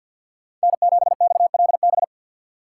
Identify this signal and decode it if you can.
Morse code, heard as a keyed tone, about 53 wpm, 700 Hz. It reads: N7XLB
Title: USA Call Signs - 5 Characters - 3x Character Spacing - 50wpm